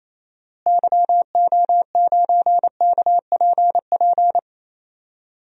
Morse code, heard as a keyed tone, about 27 wpm, 700 Hz. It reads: YO9KPP
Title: DXCC-100 Call Signs - 6 Characters - Repeated 2x - 28wpm